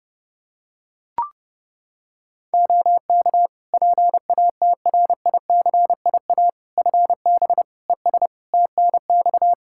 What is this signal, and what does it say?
Morse code, heard as a keyed tone, about 30 wpm, 700 Hz. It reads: OK PATRICIA FB ES TNX
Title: Practice QSO - Exchange 2 - Info - 30wpm